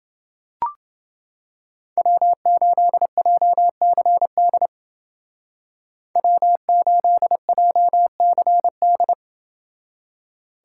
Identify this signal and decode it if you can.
Morse code, heard as a keyed tone, about 28 wpm, 700 Hz. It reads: W8JCD W8JCD